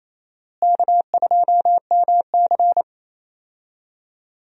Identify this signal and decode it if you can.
Morse code, heard as a keyed tone, about 28 wpm, 700 Hz. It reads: K2MC
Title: USA Call Signs - 4 Characters - Repeated 2x - 28wpm